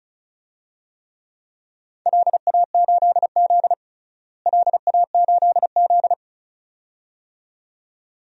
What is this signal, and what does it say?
Morse code, heard as a keyed tone, about 36 wpm, 700 Hz. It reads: LA8Z LA8Z
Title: DXCC-100 Call Signs - 4 Characters - Repeated 2x - 35wpm